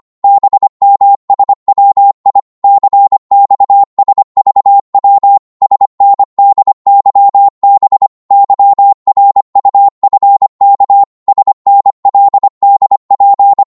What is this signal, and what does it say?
Morse code, heard as a keyed tone, about 25 wpm, 800 Hz. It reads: BMSWI CXSVW SNDYB YRUFK SNLDP
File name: Morse / ltr-25wpm